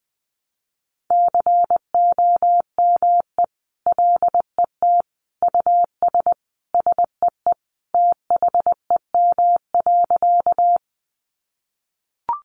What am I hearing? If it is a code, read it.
Morse code, heard as a keyed tone, about 19 wpm, 700 Hz. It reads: COME LET US SEE THEM.
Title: Sentences from Top 300 Words 20wpm